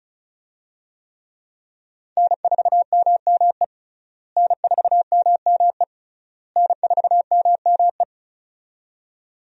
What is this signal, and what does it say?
Morse code, heard as a keyed tone, about 36 wpm, 700 Hz. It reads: N4MME N4MME N4MME
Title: USA Call Signs - 5 Characters - Repeated 3x - 35wpm